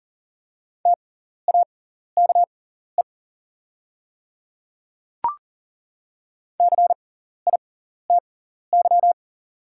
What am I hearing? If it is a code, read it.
Morse code, heard as a keyed tone, about 40 wpm, spaced as for 14 wpm, 690 Hz. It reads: TAKE CITY
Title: 6x Character Spacing - Top 300 Words - 40wpm